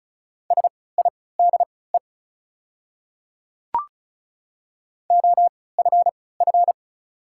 Morse code, heard as a keyed tone, about 36 wpm, spaced as for 20 wpm, 690 Hz. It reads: SIDE OFF
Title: 3x Character Spacing - Top 500 Words - 35wpm